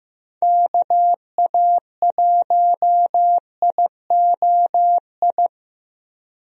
Morse code, heard as a keyed tone, about 16 wpm, 700 Hz. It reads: KA1IOI
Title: USA Call Signs - 6 Characters - 15wpm